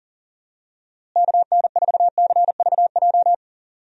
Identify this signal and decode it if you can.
Morse code, heard as a keyed tone, about 40 wpm, 700 Hz. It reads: KN4CVJ
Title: USA Call Signs - 6 Characters - Repeated 2x - 40wpm